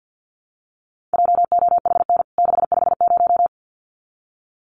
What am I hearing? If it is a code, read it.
Morse code, heard as a keyed tone, about 50 wpm, 700 Hz. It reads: JOHN 650
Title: CWT Exchange 50wpm